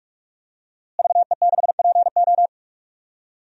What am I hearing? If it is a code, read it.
Morse code, heard as a keyed tone, about 42 wpm, 700 Hz. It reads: VE6PO
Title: DXCC-100 Call Signs - 5 Characters - Repeated 3x - 45wpm